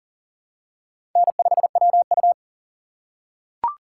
Morse code, heard as a keyed tone, about 39 wpm, 690 Hz. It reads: N5WU
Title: USA Call Signs - 4 Characters - 2x Character Spacing - 40wpm